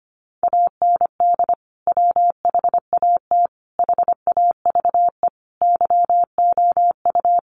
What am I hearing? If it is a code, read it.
Morse code, heard as a keyed tone, about 25 wpm, 700 Hz. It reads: AND WHAT HAVE YOU
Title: Sentences from Top 500 Words 25wpm